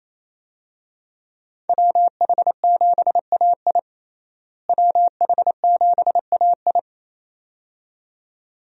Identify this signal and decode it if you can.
Morse code, heard as a keyed tone, about 28 wpm, 700 Hz. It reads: WH7AI WH7AI
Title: USA Call Signs - 5 Characters - Repeated 2x - 28wpm